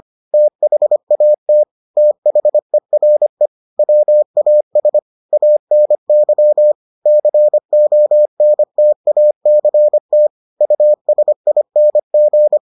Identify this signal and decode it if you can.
Morse code, heard as a keyed tone, about 25 wpm, 600 Hz. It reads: THAT THERE WAS ANY CONTACT USING